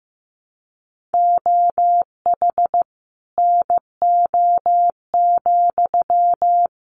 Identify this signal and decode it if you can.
Morse code, heard as a keyed tone, about 15 wpm, 700 Hz. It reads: OH NO,
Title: Sentences from Top 900 Words 15wpm